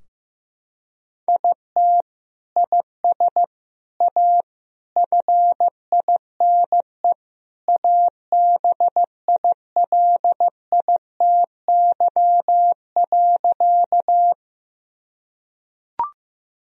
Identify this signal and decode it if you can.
Morse code, heard as a keyed tone, about 15 wpm, 700 Hz. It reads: IT IS A FINE ABILITY.
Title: Sentences from Top 1000 Words - Review Part 3 - 15wpm